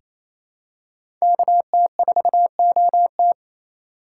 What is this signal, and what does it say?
Morse code, heard as a keyed tone, about 28 wpm, 700 Hz. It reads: KT4OT
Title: USA Call Signs - 4x Character Spacing - 28wpm